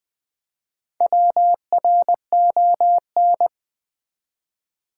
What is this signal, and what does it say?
Morse code, heard as a keyed tone, about 20 wpm, 690 Hz. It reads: WRON
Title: Build A Word - 1 Letter At A Time - 5-Letter Words - 20wpm